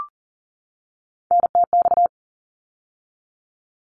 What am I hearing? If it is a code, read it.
Morse code, heard as a keyed tone, about 40 wpm, 700 Hz. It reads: NTX
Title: ARRL Sections 40wpm